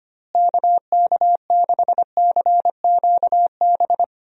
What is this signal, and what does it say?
Morse code, heard as a keyed tone, about 25 wpm, 700 Hz. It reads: KK6CQB
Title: USA Call Signs - 6 Characters - Repeated 2x - 25wpm